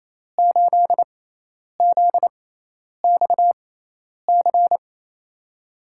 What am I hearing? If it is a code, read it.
Morse code, heard as a keyed tone, about 28 wpm, spaced as for 10 wpm, 700 Hz. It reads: 8ZXC
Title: USA Call Signs - 6 Characters - 6x Character Spacing - 28wpm